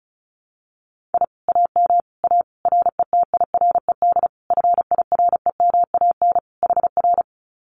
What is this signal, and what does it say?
Morse code, heard as a keyed tone, about 34 wpm, 700 Hz. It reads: I AM A RETIRED FIREMAN HR
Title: Common QSO Phrases Encore 35wpm